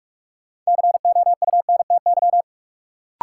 Morse code, heard as a keyed tone, about 46 wpm, 700 Hz. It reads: COUNTY
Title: Sentences from Top 1200 Words - New Word Drill Alt - 45wpm